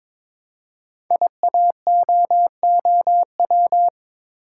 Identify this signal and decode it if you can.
Morse code, heard as a keyed tone, about 23 wpm, 700 Hz. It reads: IAOOW
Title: Code Groups - 5 Letters - 22wpm